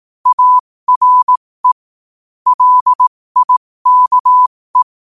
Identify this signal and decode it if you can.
Morse code, heard as a keyed tone, about 18 wpm, spaced as for 15 wpm, 1000 Hz. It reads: ARE LIKE
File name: qotdfiles / 15wpm